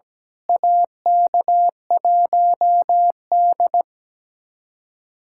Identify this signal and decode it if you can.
Morse code, heard as a keyed tone, about 17 wpm, 700 Hz. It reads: AK1D
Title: USA Callsign DE Callsign - 4 Characters - 17wpm